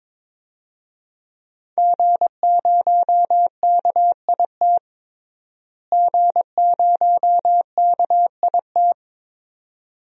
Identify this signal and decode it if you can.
Morse code, heard as a keyed tone, about 22 wpm, 700 Hz. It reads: G0KIT G0KIT
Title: DXCC-100 Call Signs - 5 Characters - Repeated 2x - 22wpm